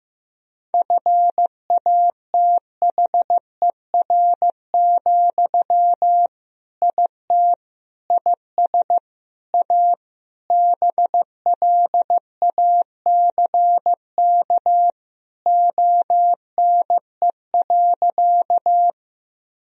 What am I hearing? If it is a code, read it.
Morse code, heard as a keyed tone, about 15 wpm, 700 Hz. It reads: FATHER, IT IS A BLACK ONE.
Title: Sentences from Top 1000 Words - Review Part 2 - 15wpm